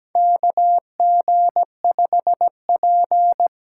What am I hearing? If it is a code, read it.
Morse code, heard as a keyed tone, about 17 wpm, 700 Hz. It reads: KG5P
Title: USA Call Signs - 8x Character Spacing - 17wpm